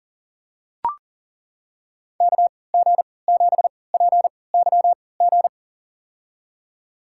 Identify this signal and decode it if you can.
Morse code, heard as a keyed tone, about 40 wpm, spaced as for 23 wpm, 700 Hz. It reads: KG7PYG
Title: USA Call Signs - 3x Character Spacing - 40wpm